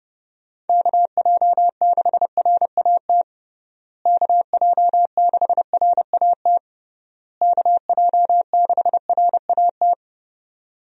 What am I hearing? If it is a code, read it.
Morse code, heard as a keyed tone, about 32 wpm, 700 Hz. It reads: KJ6RAT KJ6RAT KJ6RAT
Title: USA Call Signs - 6 Characters - Repeated 3x - 30wpm